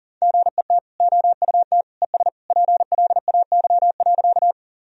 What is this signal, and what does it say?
Morse code, heard as a keyed tone, about 37 wpm, 700 Hz. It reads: GET OUT ES PLAY.
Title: Sentences from Top 200 Words and Abbreviations 40wpm